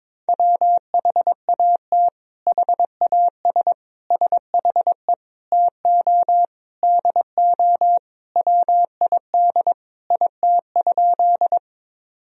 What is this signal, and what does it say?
Morse code, heard as a keyed tone, about 22 wpm, 700 Hz. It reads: WHAT HAS SHE TO DO WID IT?
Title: Sentences from Top 100 Words and Abbreviations 22wpm